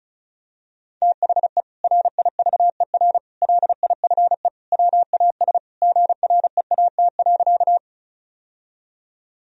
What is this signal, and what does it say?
Morse code, heard as a keyed tone, about 36 wpm, 700 Hz. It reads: THE RIVER LIFE WAS GREAT.
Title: Sentences from Top 1000 Words - Review Part 1 - 35wpm